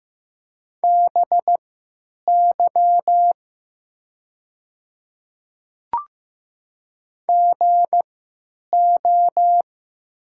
Morse code, heard as a keyed tone, about 15 wpm, spaced as for 9 wpm, 700 Hz. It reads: BY GO
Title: 3x Character Spacing - Top 200 Words - 15wpm